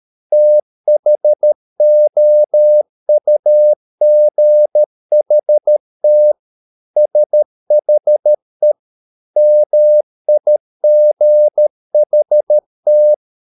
Morse code, heard as a keyed tone, about 13 wpm, 600 Hz. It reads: THOUGHT SHE MIGHT